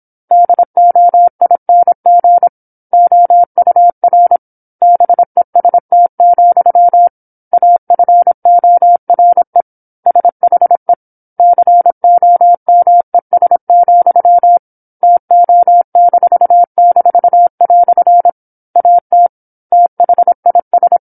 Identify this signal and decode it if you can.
Morse code, heard as a keyed tone, about 27 wpm, 700 Hz. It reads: DOING OUR BEST, AFORE SHE COMES, TO--" AT THIS
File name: 08-Chapter-08 26 WPM